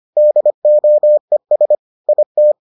Morse code, heard as a keyed tone, about 25 wpm, 600 Hz. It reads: DOES IT